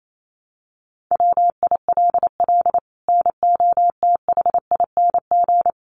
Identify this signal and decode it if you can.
Morse code, heard as a keyed tone, about 28 wpm, 700 Hz. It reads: WILL NOTHING
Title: Sentences from Top 1200 Words 28wpm